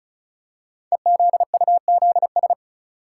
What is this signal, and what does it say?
Morse code, heard as a keyed tone, about 36 wpm, 700 Hz. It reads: EZUZS
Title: Code Groups - 5 Letters - 35wpm